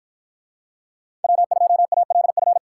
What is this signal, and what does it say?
Morse code, heard as a keyed tone, about 52 wpm, 700 Hz. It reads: W2ALF